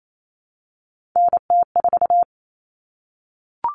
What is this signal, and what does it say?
Morse code, heard as a keyed tone, about 28 wpm, 700 Hz. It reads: NT4
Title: All Cut Numbers - 3-Digit 28wpm